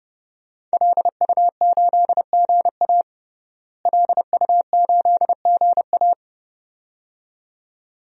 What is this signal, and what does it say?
Morse code, heard as a keyed tone, about 30 wpm, 700 Hz. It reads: LU8GA LU8GA
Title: DXCC-100 Call Signs - 5 Characters - Repeated 2x - 30wpm